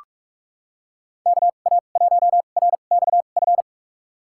Morse code, heard as a keyed tone, about 46 wpm, spaced as for 32 wpm, 700 Hz. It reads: KA1RXF
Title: USA Call Signs - 6 Characters - 2x Character Spacing - 45wpm